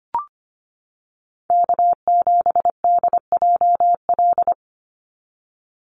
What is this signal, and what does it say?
Morse code, heard as a keyed tone, about 25 wpm, 700 Hz. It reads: K7DJL
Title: NA SOTA Chaser Callsigns 25wpm